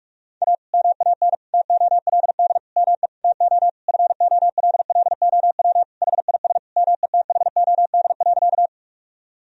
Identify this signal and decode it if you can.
Morse code, heard as a keyed tone, about 41 wpm, 700 Hz. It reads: A MAN TOLD ME TO FOLLOW HIS METHOD.
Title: Sentences from Top 1000 Words - Review Part 3 - 45wpm